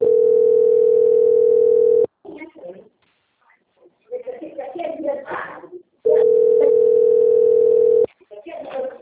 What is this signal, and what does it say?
Ringback tone (North America), 440 + 480 Hz, with speech.
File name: out- - - - - .30317